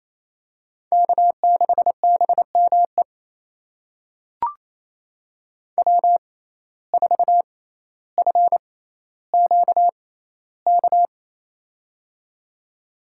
Morse code, heard as a keyed tone, about 28 wpm, spaced as for 14 wpm, 700 Hz. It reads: K6BME W4FQK
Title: USA Call Signs - 5 Characters - 6x Character Spacing - 28wpm